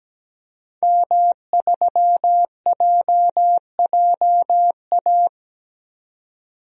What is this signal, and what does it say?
Morse code, heard as a keyed tone, about 17 wpm, 700 Hz. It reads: M3JJA
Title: DXCC-100 Call Signs - 5 Characters - 17wpm